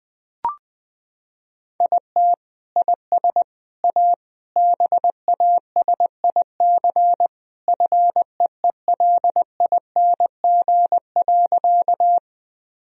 Morse code, heard as a keyed tone, about 20 wpm, 700 Hz. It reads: IT IS A BASIC FEELING.
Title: Sentences from Top 1000 Words - Review Part 3 - 20wpm